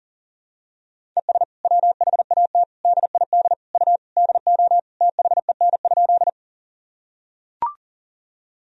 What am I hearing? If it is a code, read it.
Morse code, heard as a keyed tone, about 40 wpm, 700 Hz. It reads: ES WHAT DID U DO THEN?
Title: Sentences from Top 100 Words and Abbreviations 40wpm